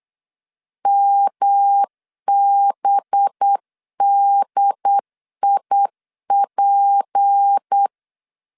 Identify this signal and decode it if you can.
Morse code, heard as a keyed tone, about 8 wpm, 800 Hz. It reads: MBDIP
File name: CW / DET / 1 6 1500C S B 8wpm 5142